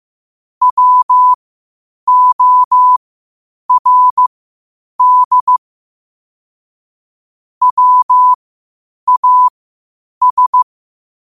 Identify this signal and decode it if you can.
Morse code, heard as a keyed tone, about 15 wpm, spaced as for 8 wpm, 1000 Hz. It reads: WORD WAS
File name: qotdfiles / 8wpm